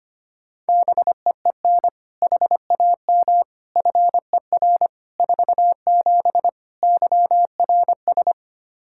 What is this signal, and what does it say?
Morse code, heard as a keyed tone, about 25 wpm, 700 Hz. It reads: BEEN HAM FER 47 YRS
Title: Practice QSO - Exchange 3 - More Info - 25wpm